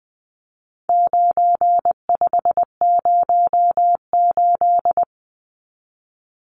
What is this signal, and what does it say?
Morse code, heard as a keyed tone, about 20 wpm, 700 Hz. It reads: 9508
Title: Numbers - 4-Digit 20wpm